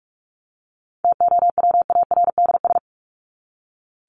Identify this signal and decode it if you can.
Morse code, heard as a keyed tone, about 44 wpm, 700 Hz. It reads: TOWARDS
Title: ES:S Suffix - Top 100 Words with Base Words 45wpm